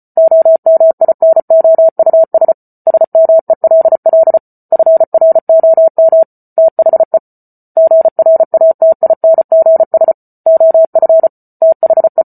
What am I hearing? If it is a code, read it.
Morse code, heard as a keyed tone, about 34 wpm, 650 Hz. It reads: OMINOUS SMELL FROM THE GRATINGS OF THE